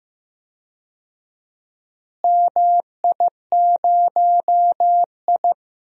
Morse code, heard as a keyed tone, about 15 wpm, 700 Hz. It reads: MI0I
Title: DXCC-100 Call Signs - 4 Characters - Repeated 2x - 15wpm